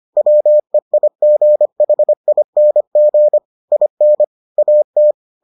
Morse code, heard as a keyed tone, about 25 wpm, 600 Hz. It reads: WEIGHING IN AT